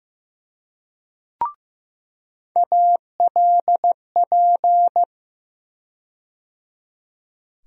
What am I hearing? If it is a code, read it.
Morse code, heard as a keyed tone, about 15 wpm, 700 Hz. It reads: ALP
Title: USA Call Sign Postfixes - 3 Letters 15wpm